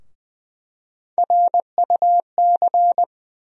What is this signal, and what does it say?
Morse code, heard as a keyed tone, about 20 wpm, 700 Hz. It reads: RUC
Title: USA Call Sign Postfixes - 3 Letters 20wpm